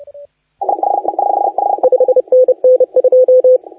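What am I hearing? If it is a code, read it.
Morse code, heard as a keyed tone, about 29 wpm, 510 Hz. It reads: 5NN 2
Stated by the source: Morse contest audio recorded by LA4C during the SAC CW 2018 contest.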